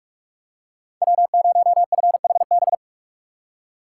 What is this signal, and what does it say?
Morse code, heard as a keyed tone, about 43 wpm, 700 Hz. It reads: W0FHB
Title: USA Call Signs - 5 Characters - 2x Character Spacing - 45wpm